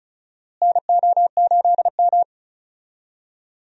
Morse code, heard as a keyed tone, about 34 wpm, 700 Hz. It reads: NO8M
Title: USA Call Signs - 4 Characters - Repeated 2x - 35wpm